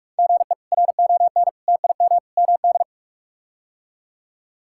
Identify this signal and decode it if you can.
Morse code, heard as a keyed tone, about 46 wpm, 700 Hz. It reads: GE RON TIM MD
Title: K1USN SST QSO Practice - 45wpm